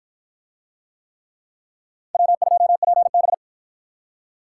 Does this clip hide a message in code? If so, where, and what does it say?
Yes: Morse code from 2.1 to 3.3 s, heard as a keyed tone, about 53 wpm, 700 Hz. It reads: W2PB